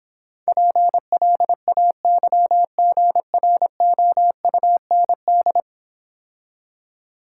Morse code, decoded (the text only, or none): PLAYGROUND